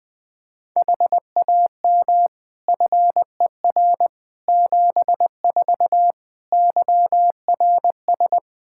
Morse code, heard as a keyed tone, about 21 wpm, 700 Hz. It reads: HAM FER 74 YRS
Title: Practice QSO - Exchange 3 - More Info - 20wpm